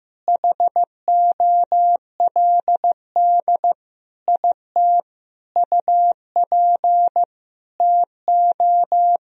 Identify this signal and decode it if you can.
Morse code, heard as a keyed tone, about 15 wpm, 700 Hz. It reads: HOLD IT UP TO
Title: Sentences from Top 1000 Words - Review Part 2 - 15wpm